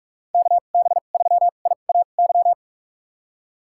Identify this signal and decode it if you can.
Morse code, heard as a keyed tone, about 42 wpm, spaced as for 31 wpm, 700 Hz. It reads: KB3IAY
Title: USA Call Signs - 6 Characters - 2x Character Spacing - 45wpm